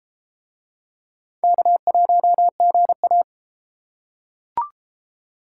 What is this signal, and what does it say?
Morse code, heard as a keyed tone, about 32 wpm, 700 Hz. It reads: K1GA